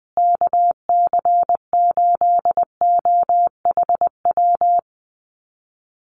Morse code, heard as a keyed tone, about 20 wpm, 700 Hz. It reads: KC8OHW
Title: USA Call Signs Repeated 3x 20wpm